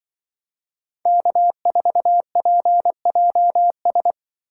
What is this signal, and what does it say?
Morse code, heard as a keyed tone, about 24 wpm, 700 Hz. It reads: K4PJS
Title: Speed Racing - USA 1x3 Call Signs 20wpm